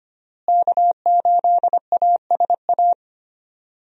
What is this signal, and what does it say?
Morse code, heard as a keyed tone, about 25 wpm, 700 Hz. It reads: K8ASA
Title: USA Call Signs - 5 Characters - Repeated 3x - 25wpm